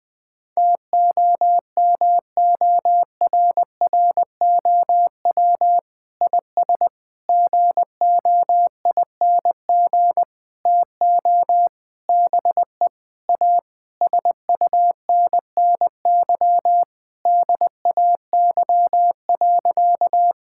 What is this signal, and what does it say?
Morse code, heard as a keyed tone, about 20 wpm, 700 Hz. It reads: TOMORROW IS GOING TO BE A SUNNY DAY.
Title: Sentences with 5-Word Phrases - Spoken Beginning - 20wpm